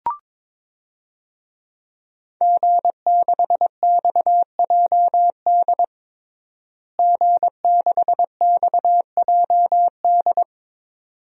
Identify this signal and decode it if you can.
Morse code, heard as a keyed tone, about 21 wpm, 700 Hz. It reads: G6XJD G6XJD
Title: DXCC-100 Call Signs - 5 Characters - Repeated 2x - 22wpm